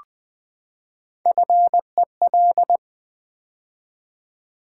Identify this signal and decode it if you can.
Morse code, heard as a keyed tone, about 20 wpm, 700 Hz. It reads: FEL
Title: Build A Word - 3 Letters At A Time - 6-Letter Words - 20wpm